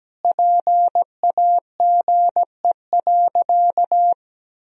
Morse code, heard as a keyed tone, about 17 wpm, 700 Hz. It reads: PAGE.
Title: Sentences from Top 1000 Words - Review Part 1 - 17wpm